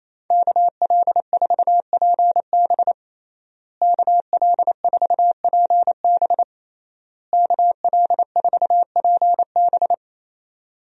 Morse code, heard as a keyed tone, about 27 wpm, 700 Hz. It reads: KL4PB KL4PB KL4PB
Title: USA Call Signs - 5 Characters - Repeated 3x - 28wpm